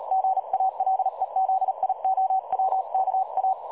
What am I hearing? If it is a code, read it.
Morse code, heard as a keyed tone, about 38 wpm, 760 Hz. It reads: GALEZIOWKA
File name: dlazaawansowanych / nagraniazpasma / websdr recording 3565.0kHz